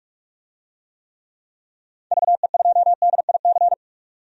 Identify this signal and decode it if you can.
Morse code, heard as a keyed tone, about 44 wpm, 690 Hz. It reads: VE2DIC
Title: DXCC-100 Call Signs - 6 Characters - 45wpm